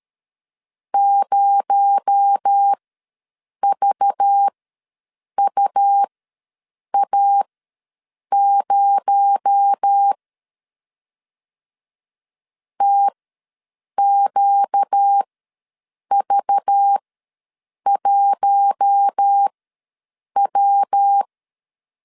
Morse code, heard as a keyed tone, about 13 wpm, spaced as for 7 wpm, 800 Hz. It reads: 0VUA0 TQV1W
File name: LEZIONI / 6 PKX0 / 6 PKX0 ws 12wpm 3448